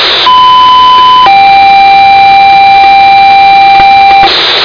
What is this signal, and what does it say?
Two-tone page, heard as a one-second tone, then a three-second tone, about 1010 then 790 Hz, with noise. It is heard from 0.3 to 4.3 s.